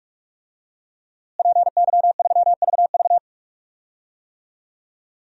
Morse code, heard as a keyed tone, about 46 wpm, 700 Hz. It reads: PY3VV
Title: DXCC-100 Call Signs - 5 Characters - Repeated 2x - 45wpm